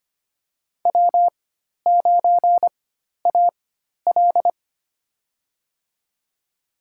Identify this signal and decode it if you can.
Morse code, heard as a keyed tone, about 25 wpm, spaced as for 12 wpm, 700 Hz. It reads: W9AL
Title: USA Call Signs - 4x Character Spacing - 25wpm